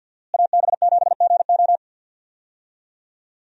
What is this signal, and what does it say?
Morse code, heard as a keyed tone, about 48 wpm, 700 Hz. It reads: AB7GO
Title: USA Call Signs - 5 Characters - Repeated 2x - 50wpm